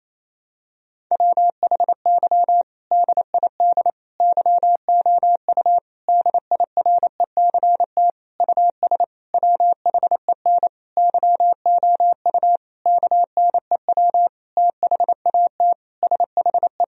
Morse code, heard as a keyed tone, about 28 wpm, 700 Hz. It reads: WHY DID YOU DIRECT US WHEN YOU KNEW THAT SHE
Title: Sentences from Top 1000 Words - Review Part 2 - 28wpm